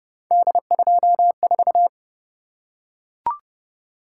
Morse code, heard as a keyed tone, about 29 wpm, 700 Hz. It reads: D24